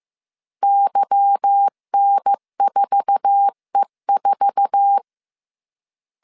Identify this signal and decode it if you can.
Morse code, heard as a keyed tone, about 15 wpm, 800 Hz. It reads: YN4E4
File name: CW / DET / 1 9 1500C S C 14wpm 3245